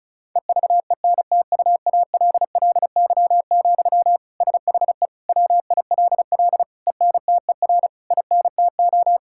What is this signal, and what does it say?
Morse code, heard as a keyed tone, about 34 wpm, 700 Hz. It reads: EVENTUALLY, SHE WILL ENTER INTO
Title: Sentences from Top 1000 Words - Review Part 6 - 35wpm